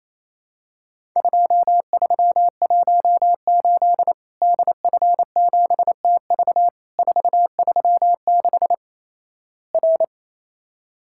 Morse code, heard as a keyed tone, about 28 wpm, 700 Hz. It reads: 2318 DF7TV 436 R
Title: Worked All Europe DX Contest - QTC Exchange - 28wpm